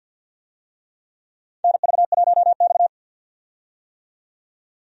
Morse code, heard as a keyed tone, about 52 wpm, 700 Hz. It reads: NV1X
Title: USA Call Signs - 4 Characters - 50wpm